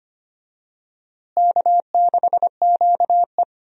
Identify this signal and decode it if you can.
Morse code, heard as a keyed tone, about 25 wpm, 700 Hz. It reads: K6QE
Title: USA Call Signs - 4 Characters - Repeated 3x - 25wpm